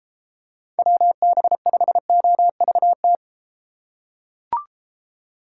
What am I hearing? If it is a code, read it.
Morse code, heard as a keyed tone, about 32 wpm, 700 Hz. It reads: WB5OVT